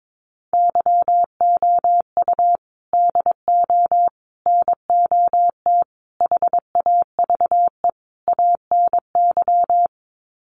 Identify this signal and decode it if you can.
Morse code, heard as a keyed tone, about 22 wpm, 700 Hz. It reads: YOU DO NOT HAVE ANY
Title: Sets of 5 Words Encore 22wpm